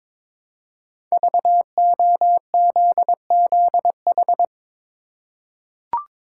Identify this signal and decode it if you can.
Morse code, heard as a keyed tone, about 23 wpm, 700 Hz. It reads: VOZZH
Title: Code Groups - 5 Letter:Numbers - 22wpm